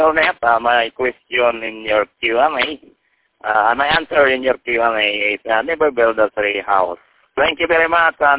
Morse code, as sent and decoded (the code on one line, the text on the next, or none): none